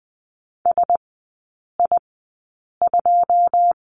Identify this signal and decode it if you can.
Morse code, heard as a keyed tone, about 19 wpm, spaced as for 8 wpm, 700 Hz. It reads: SI2
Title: 127 Lesson 21e - Number 6 - Three Characters 20wpm